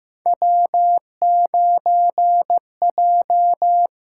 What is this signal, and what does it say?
Morse code, heard as a keyed tone, about 15 wpm, 700 Hz. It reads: W9J
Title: USA Call Signs - 4 Characters - 6x Character Spacing - 15wpm